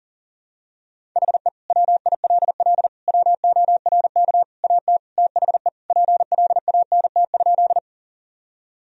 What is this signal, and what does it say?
Morse code, heard as a keyed tone, about 40 wpm, 700 Hz. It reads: HE WILL WORK AT THE PLANT?
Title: Sentences from Top 1000 Words - Review Part 1 - 40wpm